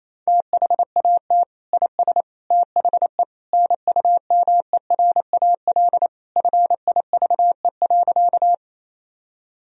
Morse code, heard as a keyed tone, about 28 wpm, 700 Hz. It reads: THAT IS THE NUMERAL FIVE.
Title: Sentences from Top 1000 Words - Review Part 2 - 28wpm